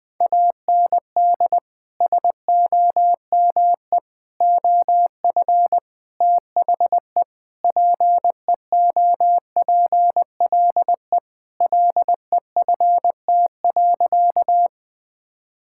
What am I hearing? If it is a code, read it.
Morse code, heard as a keyed tone, about 20 wpm, 700 Hz. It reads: AND SOME OF THE PEOPLE LEFT.
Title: Sentences with 4-Word Phrases - Spoken Beginning - 20wpm